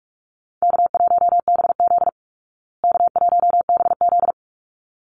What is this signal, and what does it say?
Morse code, heard as a keyed tone, about 44 wpm, 700 Hz. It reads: K1BZ K1BZ
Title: Easy Call Signs Repeated 2x - 45wpm